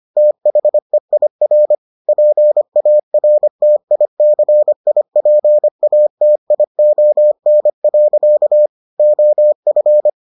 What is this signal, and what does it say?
Morse code, heard as a keyed tone, about 25 wpm, 600 Hz. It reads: THEIR PARTICIPATION. OF